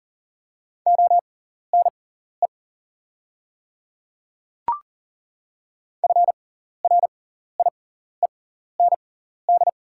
Morse code, heard as a keyed tone, about 37 wpm, spaced as for 14 wpm, 700 Hz. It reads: ONE FRIEND
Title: 6x Character Spacing - Top 500 Words - 40wpm